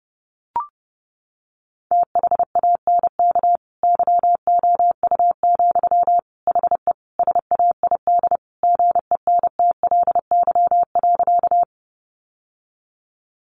Morse code, heard as a keyed tone, about 30 wpm, 700 Hz. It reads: THANK YOU, HE SAID GENTLY.
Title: Sentences from Top 1500 Words 30wpm